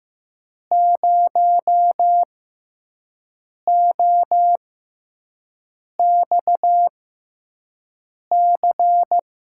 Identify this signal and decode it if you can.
Morse code, heard as a keyed tone, about 15 wpm, spaced as for 5 wpm, 700 Hz. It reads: 0OXC
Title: USA Call Signs - 6 Characters - 6x Character Spacing - 15wpm